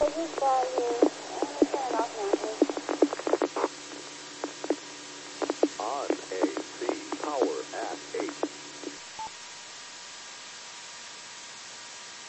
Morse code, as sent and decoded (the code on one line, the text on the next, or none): none